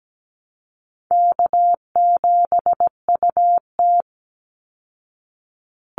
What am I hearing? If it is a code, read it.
Morse code, heard as a keyed tone, about 17 wpm, 700 Hz. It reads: K7UT
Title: Easy Call Signs Repeated 3x - 17wpm